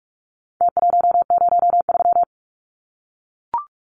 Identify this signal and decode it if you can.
Morse code, heard as a keyed tone, about 44 wpm, 700 Hz. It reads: T103
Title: Numbers - Contest Style - 4-Digit 45wpm